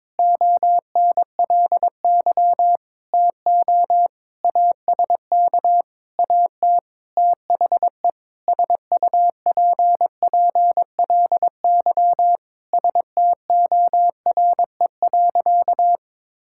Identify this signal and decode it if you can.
Morse code, heard as a keyed tone, about 22 wpm, 700 Hz. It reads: ONLY TO ASK AT THE SUPPLY STORE.
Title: Sentences from Top 1000 Words - Review Part 5 - 22wpm